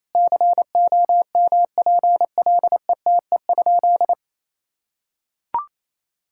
Morse code, heard as a keyed tone, about 28 wpm, 700 Hz. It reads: COMPLETE?
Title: Sentences from Top 1000 Words - Review Part 6 - 28wpm